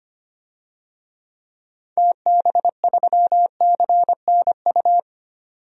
Morse code, heard as a keyed tone, about 25 wpm, 700 Hz. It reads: TB3CNU
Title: DXCC-100 Call Signs - 6 Characters - 25wpm